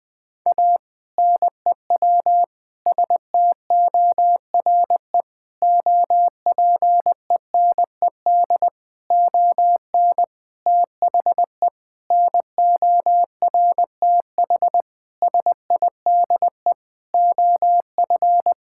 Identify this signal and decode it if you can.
Morse code, heard as a keyed tone, about 20 wpm, 700 Hz. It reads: A NEW STORE OPENED ON THE NORTH SIDE OF
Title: Sentences with 6-Word Phrases - Spoken Beginning - 20wpm